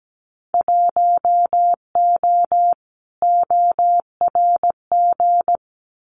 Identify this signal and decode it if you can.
Morse code, heard as a keyed tone, about 17 wpm, 700 Hz. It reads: 1O ORG
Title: Winter Field Day Exchanges 17wpm